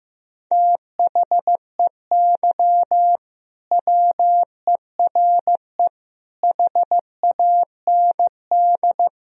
Morse code, heard as a keyed tone, about 15 wpm, 700 Hz. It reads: THEY WERE HAND
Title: Sentences from Top 1000 Words - Review Part 2 - 15wpm